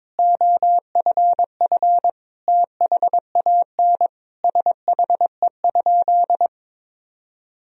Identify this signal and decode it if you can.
Morse code, heard as a keyed tone, about 22 wpm, 700 Hz. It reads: OFF THAN SHE?
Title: Sentences from Top 1000 Words - Review Part 3 - 22wpm